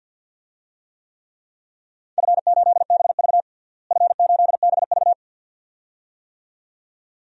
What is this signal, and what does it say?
Morse code, heard as a keyed tone, about 50 wpm, 700 Hz. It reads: F8BV F8BV
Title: DXCC-100 Call Signs - 4 Characters - Repeated 2x - 50wpm